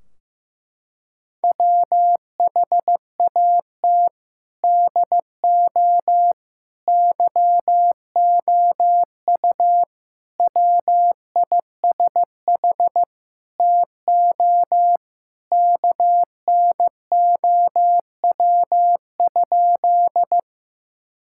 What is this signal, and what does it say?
Morse code, heard as a keyed tone, about 15 wpm, 700 Hz. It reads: WHAT DO YOU WISH TO KNOW?
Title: Sentences from Top 1600 Words 15wpm